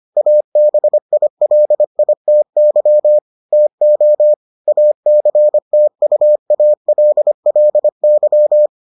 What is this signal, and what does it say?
Morse code, heard as a keyed tone, about 25 wpm, 600 Hz. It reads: ABILITY TO ACTUALLY